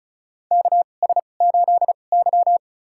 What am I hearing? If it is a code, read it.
Morse code, heard as a keyed tone, about 36 wpm, spaced as for 25 wpm, 700 Hz. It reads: KS8Y
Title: USA Call Signs - 4 Characters - 2x Character Spacing - 35wpm